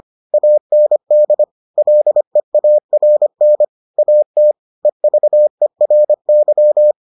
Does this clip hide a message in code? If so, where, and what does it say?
Yes: Morse code from 0.3 to 6.9 s, heard as a keyed tone, about 25 wpm, 600 Hz. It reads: AND LEARN AT EVERY